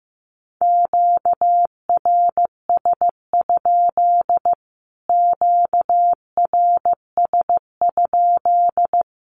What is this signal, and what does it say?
Morse code, heard as a keyed tone, about 15 wpm, 700 Hz. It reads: QRS? QRS?
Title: Q-Codes Repeated 3x 15wpm